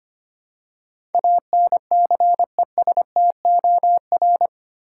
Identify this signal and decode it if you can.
Morse code, heard as a keyed tone, about 25 wpm, 700 Hz. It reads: ANCESTOR
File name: Eight Letter Words - 25wpm